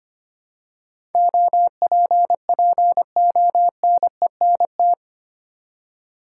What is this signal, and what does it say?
Morse code, heard as a keyed tone, about 25 wpm, 700 Hz. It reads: OPPONENT
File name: Eight Letter Words - 25wpm